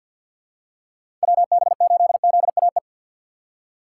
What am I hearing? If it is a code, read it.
Morse code, heard as a keyed tone, about 51 wpm, 690 Hz. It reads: WB8ZAE
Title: USA Call Signs - 6 Characters - 3x Character Spacing - 50wpm